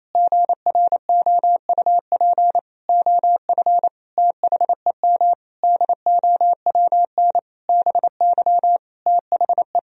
Morse code, heard as a keyed tone, about 28 wpm, 700 Hz. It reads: GROUP OF THEM DOWN BY THE
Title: Sentences from Top 1000 Words - Review Part 1 - 28wpm